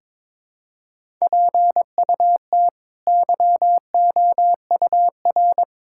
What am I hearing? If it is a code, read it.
Morse code, heard as a keyed tone, about 23 wpm, 700 Hz. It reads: PUT YOUR
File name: practice / 22wpm / Sets of 7 Words - 22wpm